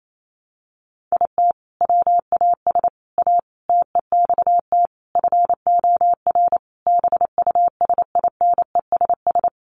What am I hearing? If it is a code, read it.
Morse code, heard as a keyed tone, about 28 wpm, 700 Hz. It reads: IT WAS A TEXT FOR BUSINESS
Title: Sentences from Top 1200 Words 28wpm